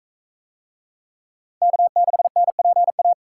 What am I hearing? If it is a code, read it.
Morse code, heard as a keyed tone, about 37 wpm, 700 Hz. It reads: K6NPA